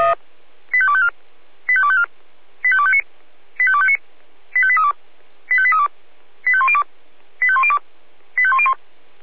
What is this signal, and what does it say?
Five-tone selective call (ZVEI), heard as quick runs of five tones, tones of about 70 ms. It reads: 85254 85254 85259 85259 85832 85832 85102 85102 85101